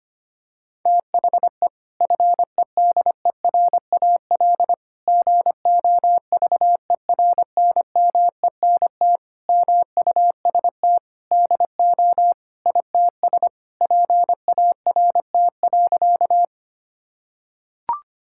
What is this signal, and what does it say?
Morse code, heard as a keyed tone, about 25 wpm, 700 Hz. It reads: THE FEDERAL GOVERNMENT MUST DO ITS PART.
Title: Sentences from Top 1000 Words - Review Part 5 - 25wpm